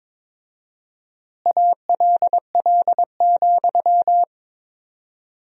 Morse code, heard as a keyed tone, about 22 wpm, 700 Hz. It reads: ALL,
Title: 4x Word-Spacing - Sentences from Top 500 Words - 22wpm